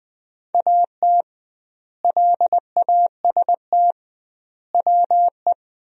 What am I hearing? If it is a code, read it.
Morse code, heard as a keyed tone, about 20 wpm, 700 Hz. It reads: AT LAST WE
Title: 2x Word-Spacing - Sentences from Top 300 Words - 20wpm